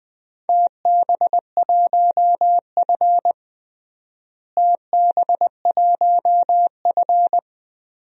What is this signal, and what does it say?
Morse code, heard as a keyed tone, about 20 wpm, 700 Hz. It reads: TB1F TB1F
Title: DXCC-100 Call Signs - 4 Characters - Repeated 3x - 20wpm